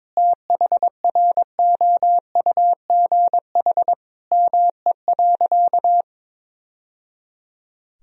Morse code, heard as a keyed tone, about 22 wpm, 700 Hz. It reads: THROUGH ME.
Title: Sentences from Top 1000 Words - Review Part 1 - 22wpm